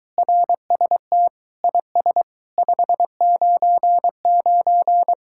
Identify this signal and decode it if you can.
Morse code, heard as a keyed tone, about 23 wpm, 700 Hz. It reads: RST IS 599